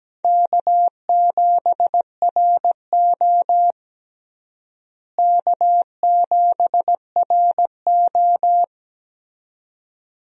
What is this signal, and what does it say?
Morse code, heard as a keyed tone, about 17 wpm, 700 Hz. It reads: K7RO K7RO
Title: USA Call Signs - 4 Characters - Repeated 2x - 17wpm